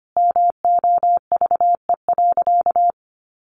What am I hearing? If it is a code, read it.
Morse code, heard as a keyed tone, about 25 wpm, 700 Hz. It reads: MOVE.
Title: Sentences from Top 1400 Words 25wpm